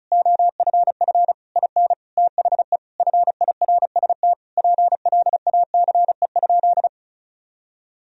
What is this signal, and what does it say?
Morse code, heard as a keyed tone, about 36 wpm, 700 Hz. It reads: OFF IN THE FIRST PLACE?
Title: Sentences from Top 200 Words and Abbreviations 35wpm